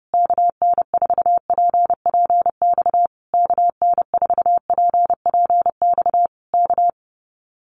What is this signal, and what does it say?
Morse code, heard as a keyed tone, about 30 wpm, 700 Hz. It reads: KN4PPX KN4PPX K
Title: Common QSO Phrases Encore 30wpm